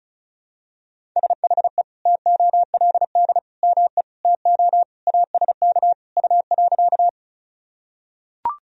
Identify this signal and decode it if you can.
Morse code, heard as a keyed tone, about 34 wpm, 700 Hz. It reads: SHE TOLD ME TO ASK U.
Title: Sentences from Top 400 Words and Abbreviations 35wpm